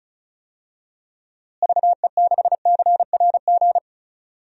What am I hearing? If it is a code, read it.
Morse code, heard as a keyed tone, about 34 wpm, 700 Hz. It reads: VE6CRG
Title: DXCC-100 Call Signs - 6 Characters - Repeated 3x - 35wpm